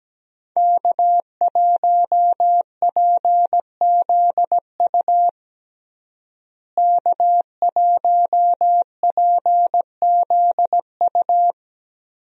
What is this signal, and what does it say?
Morse code, heard as a keyed tone, about 17 wpm, 700 Hz. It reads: K1PZU K1PZU
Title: USA Call Signs - 5 Characters - Repeated 2x - 17wpm